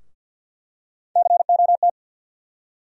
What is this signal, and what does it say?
Morse code, heard as a keyed tone, about 53 wpm, 700 Hz. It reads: COT
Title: USA Call Sign Postfixes - 3 Letters 50wpm